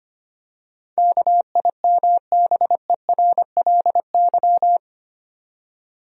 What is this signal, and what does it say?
Morse code, heard as a keyed tone, about 25 wpm, 700 Hz. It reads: KIMBERLY
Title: USA Common Names - 25wpm